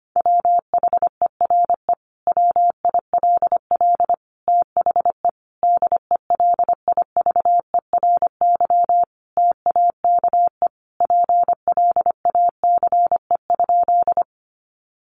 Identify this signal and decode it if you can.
Morse code, heard as a keyed tone, about 25 wpm, 700 Hz. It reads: WHERE WILL THE DELIVERY TAKE PLACE?
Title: Sentences from Top 1500 Words 25wpm